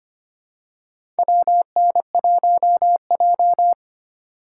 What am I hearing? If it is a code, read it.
Morse code, heard as a keyed tone, about 25 wpm, 700 Hz. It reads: WN1J
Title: USA Call Signs - 4 Characters - 3x Character Spacing - 25wpm